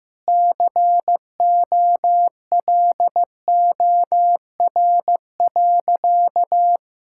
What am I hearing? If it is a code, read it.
Morse code, heard as a keyed tone, about 16 wpm, 700 Hz. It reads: COLOR.
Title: Sentences from Top 1600 Words 15wpm